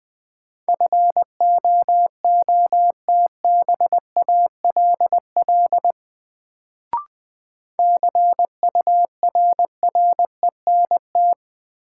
Morse code, heard as a keyed tone, about 20 wpm, 700 Hz. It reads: FOOTBALL CURRENT
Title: Sentences from Top 1600 Words - New Word Drill Alt - 20wpm